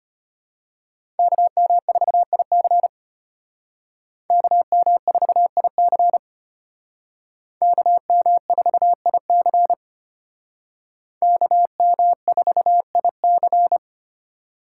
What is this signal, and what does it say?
Morse code, heard as a keyed tone, about 30 wpm, 700 Hz. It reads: KM4IC KM4IC KM4IC KM4IC
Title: Speed Racing - USA Call Signs 25wpm